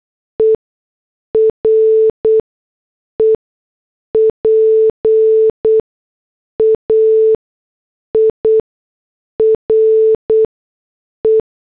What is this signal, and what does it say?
Morse code, heard as a keyed tone, about 8 wpm, 440 Hz. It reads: E R E P A I R E
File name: morse / headlines